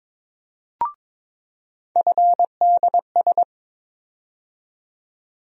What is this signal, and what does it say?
Morse code, heard as a keyed tone, about 21 wpm, 700 Hz. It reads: FDS
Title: USA Call Sign Postfixes - 3 Letters 22wpm